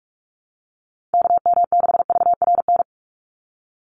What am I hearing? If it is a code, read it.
Morse code, heard as a keyed tone, about 44 wpm, 700 Hz. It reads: KM6VRN
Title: USA Call Signs 45wpm